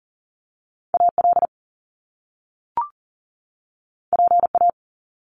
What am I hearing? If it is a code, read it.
Morse code, heard as a keyed tone, about 40 wpm, 700 Hz. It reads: AL PA
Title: Top Two Letter Combinations in Words 40wpm